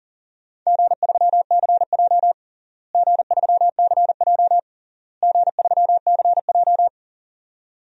Morse code, heard as a keyed tone, about 40 wpm, 700 Hz. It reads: G3CJ G3CJ G3CJ
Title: DXCC-100 Call Signs - 4 Characters - Repeated 3x - 40wpm